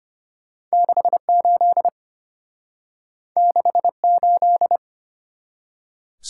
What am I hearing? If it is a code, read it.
Morse code, heard as a keyed tone, about 32 wpm, 700 Hz. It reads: 68 68